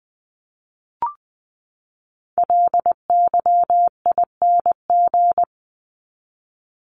Morse code, heard as a keyed tone, about 20 wpm, 700 Hz. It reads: LYING
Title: ING Suffix - Top 100 Words with Base Words 20wpm